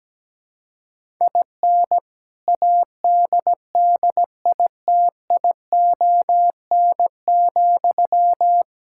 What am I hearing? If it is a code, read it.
Morse code, heard as a keyed tone, about 17 wpm, 700 Hz. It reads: IN ADDITION,
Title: Sentences from Top 1000 Words - Review Part 6 - 17wpm